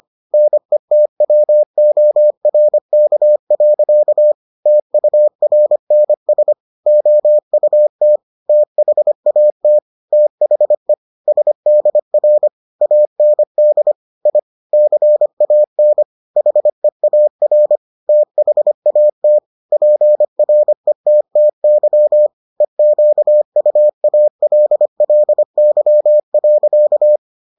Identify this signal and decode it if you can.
Morse code, heard as a keyed tone, about 25 wpm, 600 Hz. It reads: NETWORK. TURNS OUT THAT THE SDR AND I CAN HEAR THAT PRETTY EQUALLY.